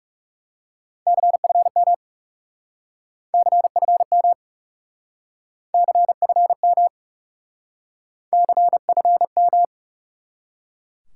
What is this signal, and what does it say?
Morse code, heard as a keyed tone, about 40 wpm, 700 Hz. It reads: CFM CFM CFM CFM